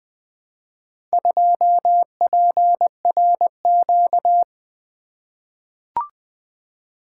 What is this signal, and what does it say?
Morse code, heard as a keyed tone, about 20 wpm, 700 Hz. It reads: 2PRQ
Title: Four Letter-Number Combinations - 20wpm